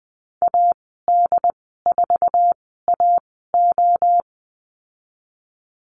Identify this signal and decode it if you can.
Morse code, heard as a keyed tone, about 20 wpm, spaced as for 15 wpm, 700 Hz. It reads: AD4AO
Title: 122 Lesson 20g - Number 3 - Call Signs 20wpm